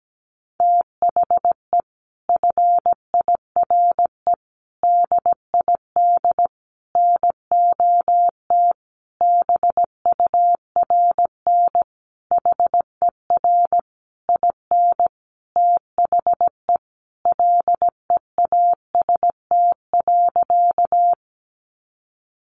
Morse code, heard as a keyed tone, about 17 wpm, 700 Hz. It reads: THE FIRE DID NOT BURN HER IN THE LEAST.
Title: Sentences from Top 1300 Words 17wpm